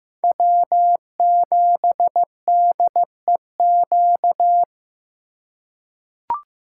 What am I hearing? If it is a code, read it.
Morse code, heard as a keyed tone, about 15 wpm, 700 Hz. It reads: W7DEQ
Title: USA Call Signs - 3x Character Spacing - 15wpm